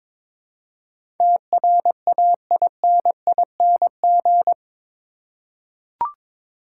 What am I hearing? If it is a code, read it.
Morse code, heard as a keyed tone, about 21 wpm, 700 Hz. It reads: TRAINING
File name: practice / 22wpm / Eight Letter Words - 22wpm